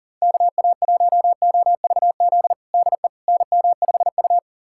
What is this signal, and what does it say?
Morse code, heard as a keyed tone, about 37 wpm, 700 Hz. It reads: KA1OVZ DE NM5U
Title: USA Callsign DE Callsign 40wpm